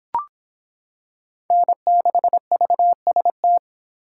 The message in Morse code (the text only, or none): N6VST